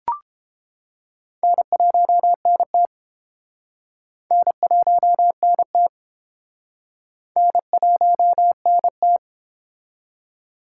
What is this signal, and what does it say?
Morse code, heard as a keyed tone, about 30 wpm, 700 Hz. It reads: N1NT N1NT N1NT